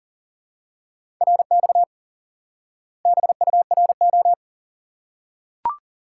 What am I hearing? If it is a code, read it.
Morse code, heard as a keyed tone, about 40 wpm, 700 Hz. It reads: RX BURO
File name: IQE Course 3 - Lesson 89 - YAGI - Two Elements - 40wpm